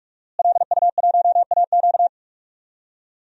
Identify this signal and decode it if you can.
Morse code, heard as a keyed tone, about 46 wpm, 700 Hz. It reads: LU1AQ
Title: DXCC-100 Call Signs - 5 Characters - Repeated 2x - 45wpm